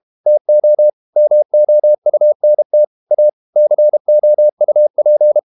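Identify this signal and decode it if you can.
Morse code, heard as a keyed tone, about 32 wpm, 600 Hz. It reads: TO MOUNT A COUP
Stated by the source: Morse code practice at 32 wpm from CW med Gnister, reading Ken Follett's novel Never in English.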